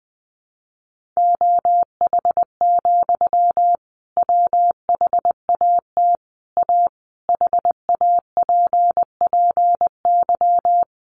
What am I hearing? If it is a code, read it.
Morse code, heard as a keyed tone, about 20 wpm, 700 Hz. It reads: OH, WHAT A HAPPY
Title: Sentences from Top 1500 Words 20wpm